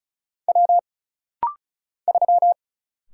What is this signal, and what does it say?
Morse code, heard as a keyed tone, about 36 wpm, 700 Hz. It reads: W 3
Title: Single Letter-Number Alt - Rapid-Fire 35wpm